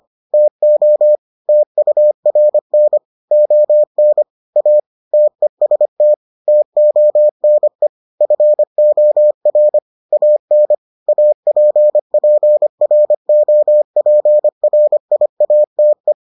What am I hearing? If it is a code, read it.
Morse code, heard as a keyed tone, about 25 wpm, 600 Hz. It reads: TO TURN ON A TEST TONE FOR AN APPROPRIATE